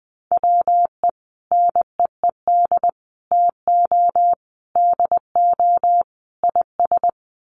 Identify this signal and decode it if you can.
Morse code, heard as a keyed tone, about 20 wpm, 700 Hz. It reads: WE NEED TO DO IS
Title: Sets of 5 Words Encore2 20wpm